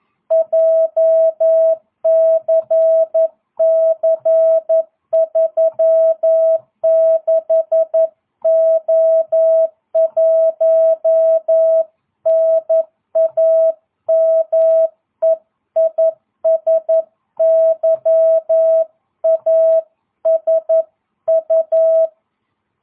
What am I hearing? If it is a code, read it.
Morse code, heard as a keyed tone, about 11 wpm, 650 Hz. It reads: JCC36O1 NAME IS YASU